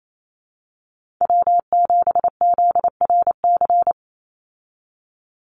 Morse code, heard as a keyed tone, about 28 wpm, 700 Hz. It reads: W7ZRC
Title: Call Signs Repeated 3x - 28wpm